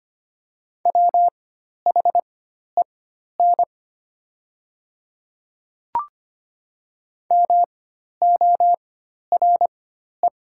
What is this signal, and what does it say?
Morse code, heard as a keyed tone, about 25 wpm, spaced as for 12 wpm, 700 Hz. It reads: WHEN MORE
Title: 4x Character Spacing - Top 200 Words - 25wpm